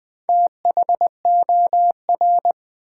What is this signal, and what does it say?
Morse code, heard as a keyed tone, about 20 wpm, 700 Hz. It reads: THOR
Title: Build A Word - 2 Letters At A Time - 8-Letter Words - 20wpm